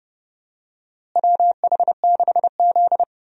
Morse code, heard as a keyed tone, about 30 wpm, 700 Hz. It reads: WH6Z
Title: USA Call Signs - 4 Characters - Repeated 2x - 30wpm